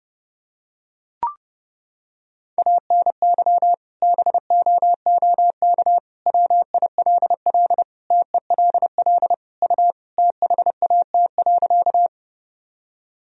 Morse code, heard as a keyed tone, about 30 wpm, 700 Hz. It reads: ANY BOOK WILL TELL U THAT.
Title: Sentences from Top 300 Words and Abbreviations 30wpm